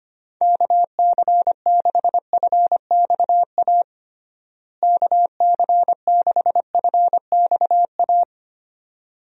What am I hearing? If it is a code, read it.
Morse code, heard as a keyed tone, about 25 wpm, 700 Hz. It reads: KC6FXA KC6FXA
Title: USA Call Signs - 6 Characters - Repeated 2x - 25wpm